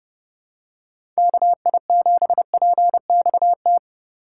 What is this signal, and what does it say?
Morse code, heard as a keyed tone, about 30 wpm, 700 Hz. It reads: KI7PXT
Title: USA Call Signs - 6 Characters - 4x Character Spacing - 30wpm